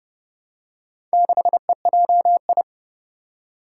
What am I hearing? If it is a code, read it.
Morse code, heard as a keyed tone, about 30 wpm, 700 Hz. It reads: 6EJI
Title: Four Challenging Letter-Number Combinations 30wpm